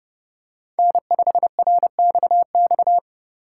Morse code, heard as a keyed tone, about 30 wpm, 700 Hz. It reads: N5RXX
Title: USA Call Signs - 5 Characters - 4x Character Spacing - 30wpm